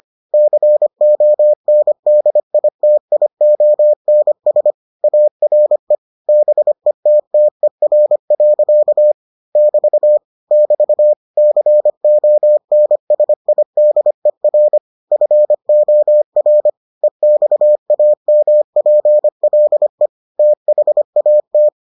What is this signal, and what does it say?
Morse code, heard as a keyed tone, about 25 wpm, 600 Hz. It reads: CONDITIONS ARE BETTER. <BT> <BT> CONSIDER FOR EXAMPLE THAT